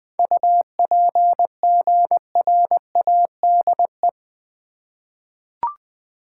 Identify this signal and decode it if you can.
Morse code, heard as a keyed tone, about 20 wpm, 700 Hz. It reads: UPGRADE
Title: 7 Letter Words - First 3 Letters Given - 20wpm